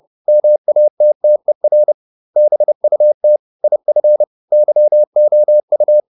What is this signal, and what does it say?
Morse code, heard as a keyed tone, about 30 wpm, 600 Hz. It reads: MATTER BUT IF YOU